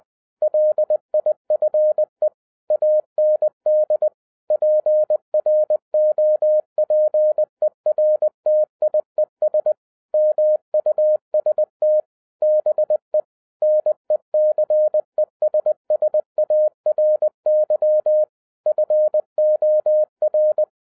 Morse code, heard as a keyed tone, about 20 wpm, 610 Hz. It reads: LIFE AND PROPERTIES MUST BE NECESSARY FOR
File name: 2011-07-mp3 / n58 BACK FROM A BIG ACCIDENT 20wpm